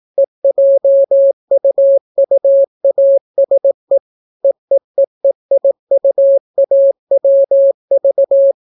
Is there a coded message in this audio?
Yes: Morse code from 0.2 to 8.5 s, heard as a keyed tone, about 18 wpm, 550 Hz. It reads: EJUUASE EEEEIUAWV